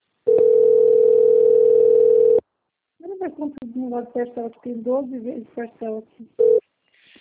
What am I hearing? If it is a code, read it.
Ringback tone (North America), 440 + 480 Hz, with speech.
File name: out- - - - - .97510